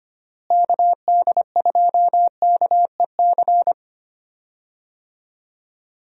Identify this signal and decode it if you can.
Morse code, heard as a keyed tone, about 25 wpm, 700 Hz. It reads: KD2KEC
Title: USA Call Signs - 6 Characters - Repeated 2x - 25wpm